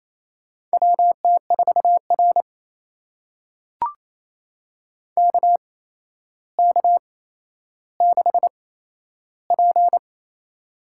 Morse code, heard as a keyed tone, about 28 wpm, spaced as for 12 wpm, 700 Hz. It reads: WT4R KK6P
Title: USA Call Signs - 4 Characters - 8x Character Spacing - 28wpm